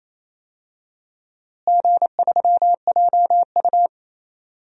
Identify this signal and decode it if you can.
Morse code, heard as a keyed tone, about 28 wpm, 700 Hz. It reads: G3JU
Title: DXCC-100 Call Signs - 4 Characters - Repeated 2x - 28wpm